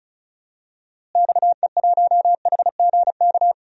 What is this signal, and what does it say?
Morse code, heard as a keyed tone, about 34 wpm, 700 Hz. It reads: XE1HGK
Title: DXCC-100 Call Signs - 6 Characters - Repeated 3x - 35wpm